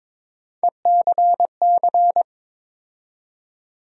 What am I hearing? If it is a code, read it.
Morse code, heard as a keyed tone, about 22 wpm, 700 Hz. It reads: ECC
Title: USA Call Sign Postfixes - 3 Letters 22wpm